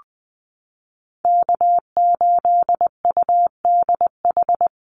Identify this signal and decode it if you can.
Morse code, heard as a keyed tone, about 20 wpm, 700 Hz. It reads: K8UDH
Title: Call Signs Repeated 3x - 20wpm